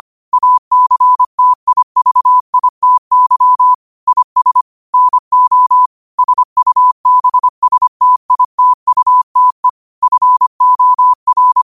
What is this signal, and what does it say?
Morse code, heard as a keyed tone, about 25 wpm, 1000 Hz. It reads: ACTIVITY IS NO SUBSTITUTE FOR